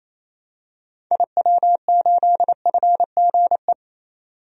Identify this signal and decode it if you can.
Morse code, heard as a keyed tone, about 28 wpm, 700 Hz. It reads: IW8FGE
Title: DXCC-100 Call Signs - 6 Characters - 28wpm